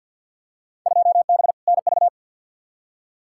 Morse code, heard as a keyed tone, about 52 wpm, 700 Hz. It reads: 2B NV
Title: Field Day Exchanges Encore - 50wpm